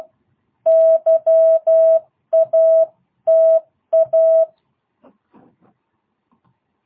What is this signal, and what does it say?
Morse code, heard as a keyed tone, about 12 wpm, 660 Hz. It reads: YATA